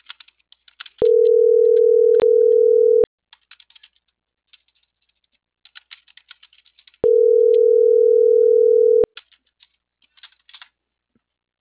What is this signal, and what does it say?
Ringback tone (North America), 440 + 480 Hz.